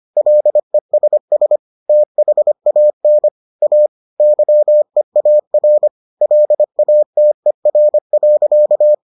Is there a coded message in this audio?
Yes: Morse code from 0.2 to 8.9 s, heard as a keyed tone, about 25 wpm, 600 Hz. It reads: LESS THAN A YEAR LATER.